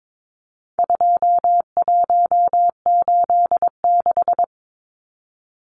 Morse code, heard as a keyed tone, about 22 wpm, 700 Hz. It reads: 2186
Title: Numbers - 4-Digit 22wpm